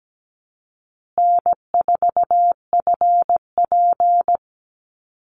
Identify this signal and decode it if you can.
Morse code, heard as a keyed tone, about 17 wpm, 700 Hz. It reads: N4FP
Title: Easy Call Signs Repeated 2x - 17wpm